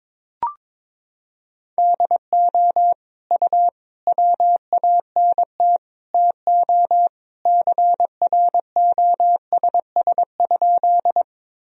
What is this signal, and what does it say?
Morse code, heard as a keyed tone, about 22 wpm, 700 Hz. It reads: DO U WANT TO CROSS?
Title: Sentences from Top 300 Words and Abbreviations 22wpm